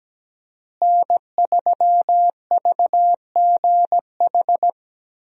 Morse code, heard as a keyed tone, about 17 wpm, 700 Hz. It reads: N3VGH
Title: USA Call Signs - 5 Characters - Repeated 3x - 17wpm